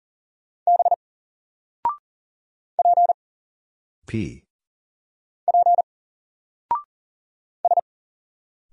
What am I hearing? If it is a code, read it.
Morse code, heard as a keyed tone, about 40 wpm, 700 Hz. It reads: B P P S
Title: Single Letter-Number Alt 40wpm